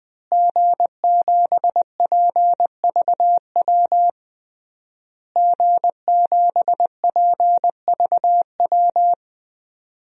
Morse code, heard as a keyed tone, about 20 wpm, 700 Hz. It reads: G7PVW G7PVW
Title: DXCC-100 Call Signs - 5 Characters - Repeated 3x - 20wpm